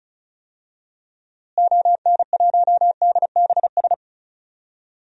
Morse code, heard as a keyed tone, about 36 wpm, 700 Hz. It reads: ON1DBS
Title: DXCC-100 Call Signs - 6 Characters - Repeated 2x - 35wpm